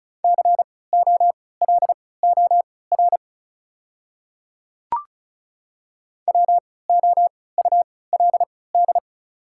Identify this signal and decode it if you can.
Morse code, heard as a keyed tone, about 34 wpm, spaced as for 20 wpm, 700 Hz. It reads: COLOR WOULD
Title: 3x Character Spacing - Top 500 Words - 35wpm